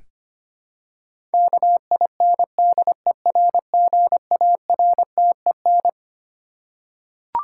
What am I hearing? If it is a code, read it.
Morse code, heard as a keyed tone, about 25 wpm, 700 Hz. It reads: KINDERGARTEN
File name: Twelve Letter Words - 25wpm